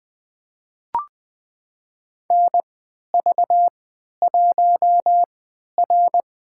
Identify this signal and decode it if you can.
Morse code, heard as a keyed tone, about 21 wpm, spaced as for 12 wpm, 700 Hz. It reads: NV1R
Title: USA Call Signs - 4 Characters - 3x Character Spacing - 20wpm